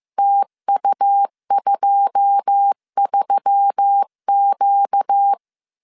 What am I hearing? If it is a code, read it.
Morse code, heard as a keyed tone, about 15 wpm, 800 Hz. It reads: TU23Q
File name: CW / DET / 1 9 1500C S C 14wpm 3245